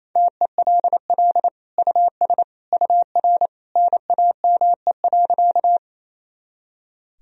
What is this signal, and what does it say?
Morse code, heard as a keyed tone, about 28 wpm, 700 Hz. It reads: TELL US UR NAME.
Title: Sentences from Top 200 Words and Abbreviations 28wpm